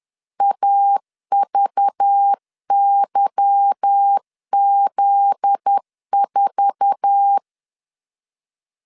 Morse code, heard as a keyed tone, about 10 wpm, 800 Hz. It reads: AVYZ4
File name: LEZIONI / 9 2867 10wpm 2733